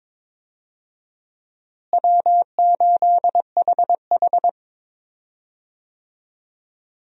Morse code, heard as a keyed tone, about 22 wpm, 700 Hz. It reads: W8HH
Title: USA Call Signs - 4 Characters - Repeated 3x - 22wpm